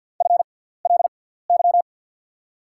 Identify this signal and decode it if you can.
Morse code, heard as a keyed tone, about 48 wpm, spaced as for 17 wpm, 690 Hz. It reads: FLY
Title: 6x Character Spacing - Top 500 Words - 50wpm